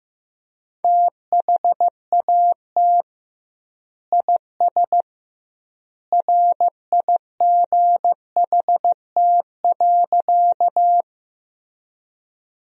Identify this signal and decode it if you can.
Morse code, heard as a keyed tone, about 15 wpm, 700 Hz. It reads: THAT IS RIGHT.
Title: 2x Word-Spacing - Sentences from Top 200 Words - 15wpm